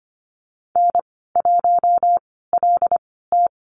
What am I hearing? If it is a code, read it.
Morse code, heard as a keyed tone, about 25 wpm, spaced as for 16 wpm, 690 Hz. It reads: N1LT
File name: Lesson 15g - Number 5 - Call Signs 25wpm